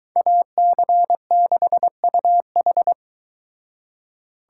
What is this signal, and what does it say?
Morse code, heard as a keyed tone, about 22 wpm, 700 Hz. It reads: AC6UH